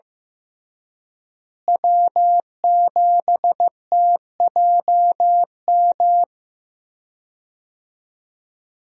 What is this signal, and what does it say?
Morse code, heard as a keyed tone, about 16 wpm, 700 Hz. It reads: W7TJM
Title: USA Call Signs - 5 Characters - Repeated 3x - 15wpm